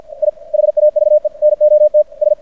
Morse code, heard as a keyed tone, about 50 wpm, 600 Hz. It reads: I HAVE NOT S
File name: archived / CRSnet